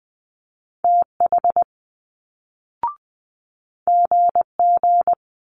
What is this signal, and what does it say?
Morse code, heard as a keyed tone, about 20 wpm, 700 Hz. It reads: TH GG
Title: Multi-letter Phonemes Quick Fire 20wpm